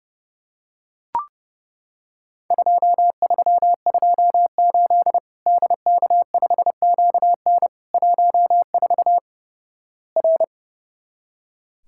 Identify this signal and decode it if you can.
Morse code, heard as a keyed tone, about 32 wpm, 700 Hz. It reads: 2328 DK5QN 14 R
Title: Worked All Europe DX Contest - QTC Exchange - 30wpm